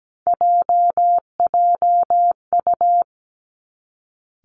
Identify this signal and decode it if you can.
Morse code, heard as a keyed tone, about 17 wpm, 700 Hz. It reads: JJU
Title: Three Challenging Letter-Number Combinations 17wpm